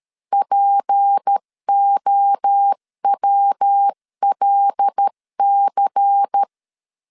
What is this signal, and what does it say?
Morse code, heard as a keyed tone, about 13 wpm, 800 Hz. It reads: POWLC
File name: CW / DET / 1 6 1500C S C 12wpm 3533